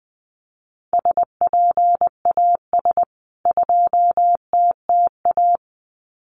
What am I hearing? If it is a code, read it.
Morse code, heard as a keyed tone, about 20 wpm, 700 Hz. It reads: SPAS 2TTA
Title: CWT Exchange 20wpm